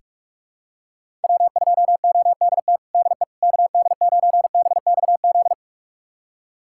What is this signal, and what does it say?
Morse code, heard as a keyed tone, about 46 wpm, 700 Hz. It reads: W2ODT DE KD9BXZ
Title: USA Callsign DE Callsign 45wpm